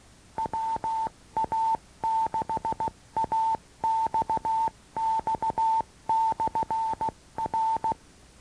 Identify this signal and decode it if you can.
Morse code, heard as a keyed tone, about 16 wpm, 890 Hz. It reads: WA6AXX/R
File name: acc / acc-rc-85-and-rc-96-instruction-tape